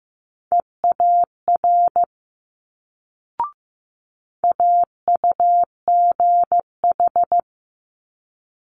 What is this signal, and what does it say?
Morse code, heard as a keyed tone, about 15 wpm, 700 Hz. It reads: EAR AUGH
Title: Multi-letter Phonemes Quick Fire 15wpm